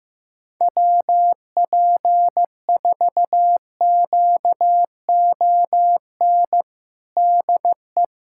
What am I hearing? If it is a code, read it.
Morse code, heard as a keyed tone, about 15 wpm, 700 Hz. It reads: WP4QON DE
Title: USA Callsign DE Callsign 15wpm